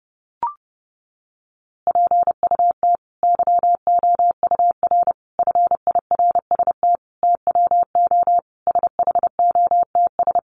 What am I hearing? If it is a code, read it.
Morse code, heard as a keyed tone, about 30 wpm, 700 Hz. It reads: PUT YOUR FIRST TWO SHOTS
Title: A Sound Of Thunder 30wpm